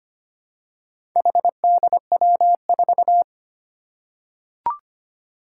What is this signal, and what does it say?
Morse code, heard as a keyed tone, about 25 wpm, 700 Hz. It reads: HDW4
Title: Four Challenging Letter-Number Combinations 25wpm